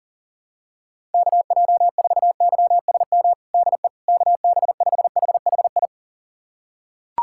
Morse code, heard as a keyed tone, about 40 wpm, 700 Hz. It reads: KJ4YSM DE KB5HHI
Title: USA Callsign DE Callsign - 6 Characters - 40wpm